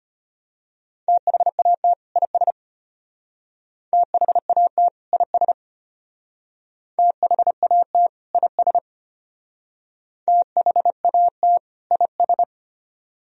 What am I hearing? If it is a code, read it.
Morse code, heard as a keyed tone, about 31 wpm, 700 Hz. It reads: THAT IS THAT IS THAT IS THAT IS
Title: Speed-Racing - Sets of 2 Words 25wpm